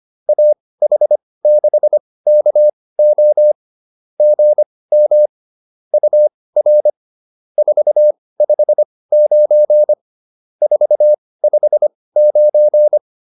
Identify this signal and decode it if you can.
Morse code, heard as a keyed tone, about 25 wpm, spaced as for 18 wpm, 600 Hz. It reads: AH6KO GM UR 459 459